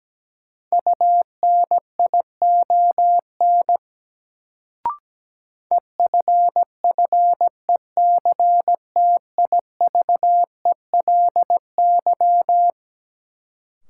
Morse code, heard as a keyed tone, about 17 wpm, 700 Hz. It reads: UNION EFFECTIVELY
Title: Sentences from Top 1400 Words - New Word Drill Alt - 17wpm